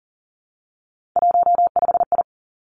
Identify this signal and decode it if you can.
Morse code, heard as a keyed tone, about 42 wpm, 700 Hz. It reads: 15I
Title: Three Challenging Letter-Number Combinations 40wpm